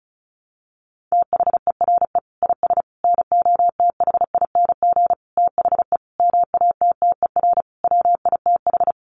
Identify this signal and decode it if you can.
Morse code, heard as a keyed tone, about 34 wpm, 700 Hz. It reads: THERE IS NOTHING THE MATTER WITH
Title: Sentences from Top 700 Words 35wpm